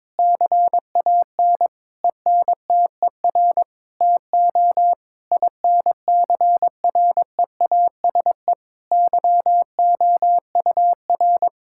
Morse code, heard as a keyed tone, about 21 wpm, 700 Hz. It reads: CAN ENTER TO INCREASE YOUR
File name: practice / 22wpm / Sets of 8 Words - 22wpm